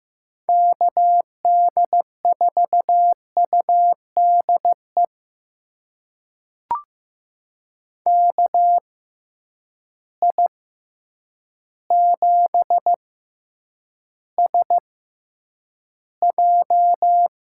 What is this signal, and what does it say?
Morse code, heard as a keyed tone, about 15 wpm, spaced as for 8 wpm, 700 Hz. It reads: KD4UDE KI7SJ
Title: USA Call Signs - 6 Characters - 6x Character Spacing - 15wpm